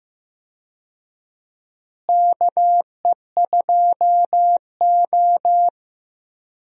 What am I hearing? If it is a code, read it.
Morse code, heard as a keyed tone, about 15 wpm, 700 Hz. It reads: KE2O
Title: USA Call Signs - 4 Characters - Repeated 2x - 15wpm